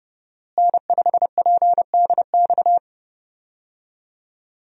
Morse code, heard as a keyed tone, about 30 wpm, 700 Hz. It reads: N5PDX
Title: USA Call Signs - 5 Characters - Repeated 3x - 30wpm